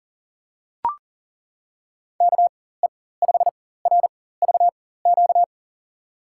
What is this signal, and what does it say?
Morse code, heard as a keyed tone, about 40 wpm, spaced as for 19 wpm, 700 Hz. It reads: KE5RVQ
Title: USA Call Signs - 6 Characters - 4x Character Spacing - 40wpm